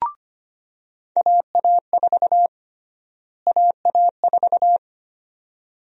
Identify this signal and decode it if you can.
Morse code, heard as a keyed tone, about 25 wpm, 700 Hz. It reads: AA4 AA4
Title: USA Call Sign Prefixes - Repeated 2x - 25wpm